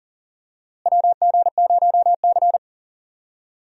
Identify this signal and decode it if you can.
Morse code, heard as a keyed tone, about 40 wpm, 700 Hz. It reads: WG0C
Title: USA Call Signs - 4 Characters - Repeated 2x - 40wpm